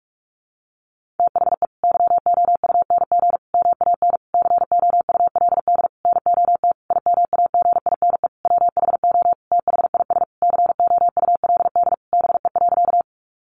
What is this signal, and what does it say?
Morse code, heard as a keyed tone, about 44 wpm, 700 Hz. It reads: THE YOUNG MAN COULD NOT IMAGINE WHO THIS COULD BE.
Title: Sentences from Top 1400 Words 45wpm